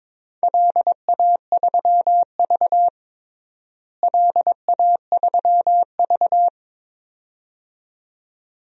Morse code, heard as a keyed tone, about 22 wpm, 700 Hz. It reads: LA3V LA3V
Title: DXCC-100 Call Signs - 4 Characters - Repeated 2x - 22wpm